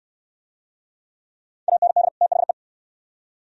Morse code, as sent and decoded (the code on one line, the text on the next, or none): .- -. -.. / - .... .
AND THE